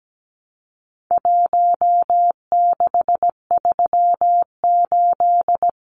Morse code, heard as a keyed tone, about 17 wpm, 700 Hz. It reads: 1638
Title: Numbers - Contest Style - 4-Digit 17wpm